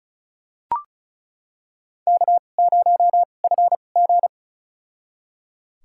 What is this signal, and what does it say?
Morse code, heard as a keyed tone, about 36 wpm, spaced as for 25 wpm, 700 Hz. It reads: K0FG
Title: USA Call Signs - 4 Characters - 2x Character Spacing - 35wpm